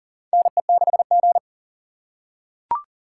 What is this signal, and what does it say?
Morse code, heard as a keyed tone, about 40 wpm, 700 Hz. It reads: NE6G
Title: USA Call Signs - 4 Characters - 2x Character Spacing - 40wpm